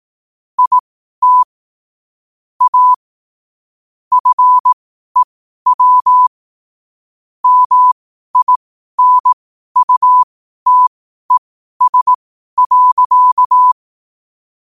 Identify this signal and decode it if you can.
Morse code, heard as a keyed tone, about 18 wpm, spaced as for 12 wpm, 1000 Hz. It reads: IT A FEW MINUTES.
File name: qotdfiles / 12wpm